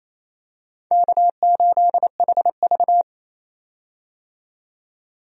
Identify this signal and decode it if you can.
Morse code, heard as a keyed tone, about 28 wpm, 700 Hz. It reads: K8HV
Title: USA Call Signs - 4 Characters - 28wpm